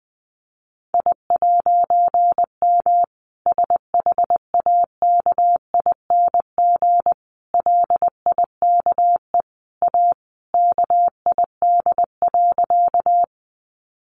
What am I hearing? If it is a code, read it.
Morse code, heard as a keyed tone, about 20 wpm, 700 Hz. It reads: I'M SHAKING LIKE A KID.
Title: A Sound Of Thunder 20wpm